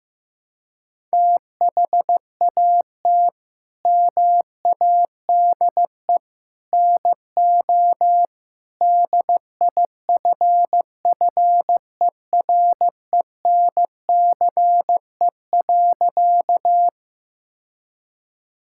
Morse code, heard as a keyed tone, about 16 wpm, 700 Hz. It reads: THAT MADE NO DIFFERENCE.
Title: Sentences from Top 1000 Words - Review Part 5 - 15wpm